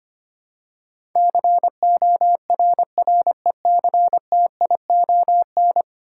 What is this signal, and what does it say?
Morse code, heard as a keyed tone, about 25 wpm, 700 Hz. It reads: CORRECTION
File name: 10 Letter Words - First 3 Letters Given - 25wpm